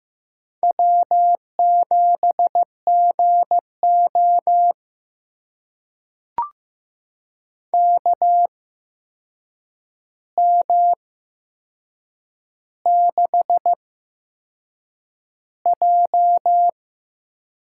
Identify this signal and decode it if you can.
Morse code, heard as a keyed tone, about 15 wpm, spaced as for 6 wpm, 700 Hz. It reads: W7GO KM6J
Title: USA Call Signs - 8x Character Spacing - 15wpm